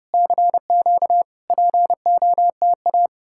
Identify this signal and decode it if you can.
Morse code, heard as a keyed tone, about 35 wpm, 700 Hz. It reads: CQ POTA